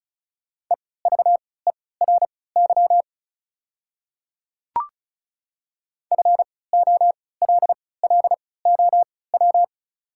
Morse code, heard as a keyed tone, about 36 wpm, spaced as for 20 wpm, 700 Hz. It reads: EVERY FOLLOW
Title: 3x Character Spacing - Top 200 Words - 35wpm